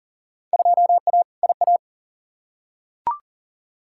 Morse code, heard as a keyed tone, about 40 wpm, 700 Hz. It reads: 2A IA
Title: Field Day Exchanges - 40wpm